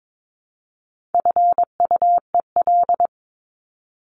Morse code, heard as a keyed tone, about 22 wpm, 700 Hz. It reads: FUEL
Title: Top 1751-2000 Words 22wpm